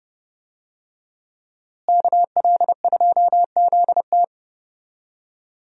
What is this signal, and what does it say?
Morse code, heard as a keyed tone, about 29 wpm, 700 Hz. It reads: KL2ZT
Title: USA Call Signs - 5 Characters - Repeated 2x - 30wpm